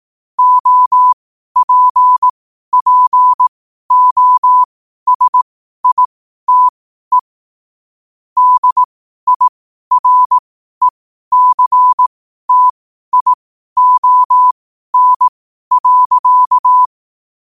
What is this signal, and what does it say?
Morse code, heard as a keyed tone, about 18 wpm, spaced as for 12 wpm, 1000 Hz. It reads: OPPOSITE DIRECTION.
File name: qotdfiles / 12wpm